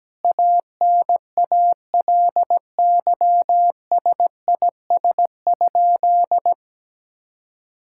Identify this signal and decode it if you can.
Morse code, heard as a keyed tone, about 17 wpm, 700 Hz. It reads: ANALYSIS?
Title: Sentences from Top 1000 Words - Review Part 5 - 17wpm